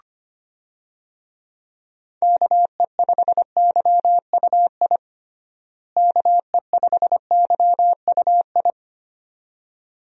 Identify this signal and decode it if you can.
Morse code, heard as a keyed tone, about 25 wpm, 700 Hz. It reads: KE5YUI KE5YUI
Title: USA Call Signs - 6 Characters - Repeated 2x - 25wpm